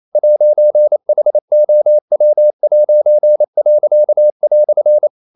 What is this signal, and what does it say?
Morse code, heard as a keyed tone, about 28 wpm, 600 Hz. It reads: 'HOW'."
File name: CW / Practice / EnglishQuotations0228